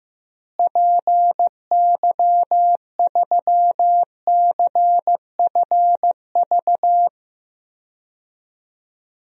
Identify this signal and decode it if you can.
Morse code, heard as a keyed tone, about 15 wpm, 700 Hz. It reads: PY3CFV
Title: DXCC-100 Call Signs - 6 Characters - Repeated 2x - 15wpm